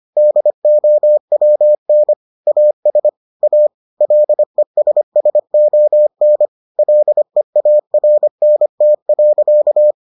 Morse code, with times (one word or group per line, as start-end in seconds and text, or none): 0.2-2.1 DOWN
2.5-3.1 AS
3.4-3.7 A
4.0-6.5 LESSON
6.8-9.9 LEARNT.